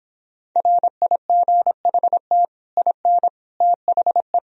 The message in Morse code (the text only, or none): RIGHT IN THE